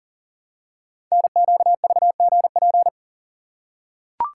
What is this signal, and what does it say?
Morse code, heard as a keyed tone, about 42 wpm, 700 Hz. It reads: NQVGP
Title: Code Groups - 5 Letters - 40wpm